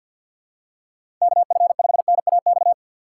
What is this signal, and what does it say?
Morse code, heard as a keyed tone, about 52 wpm, 700 Hz. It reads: KF5NAX
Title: USA Call Signs - 6 Characters - 4x Character Spacing - 50wpm